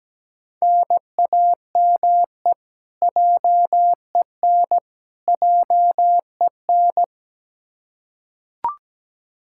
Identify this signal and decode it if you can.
Morse code, heard as a keyed tone, about 17 wpm, 700 Hz. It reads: NAME JEN JEN
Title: Practice QSO - Exchange 1 - Beginning - 17wpm